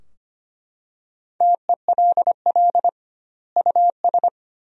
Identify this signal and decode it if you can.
Morse code, heard as a keyed tone, about 25 wpm, 700 Hz. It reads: TELL US
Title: 2x Word-Spacing - Sentences from Top 200 Words - 25wpm